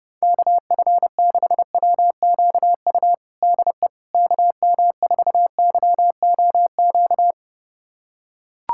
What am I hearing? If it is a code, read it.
Morse code, heard as a keyed tone, about 32 wpm, 700 Hz. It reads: KF6WQU DE KM4YOQ
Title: USA Callsign DE Callsign - 6 Characters - 30wpm